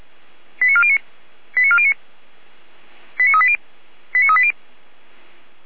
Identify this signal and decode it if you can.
Five-tone selective call (ZVEI), heard as quick runs of five tones, tones of about 70 ms. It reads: 89409 89409 89380 89380